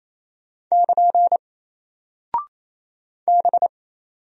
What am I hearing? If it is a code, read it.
Morse code, heard as a keyed tone, about 28 wpm, 700 Hz. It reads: <KN> B
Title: Single Character (Letter:Number:Punctuation:Prosign) Alt 28wpm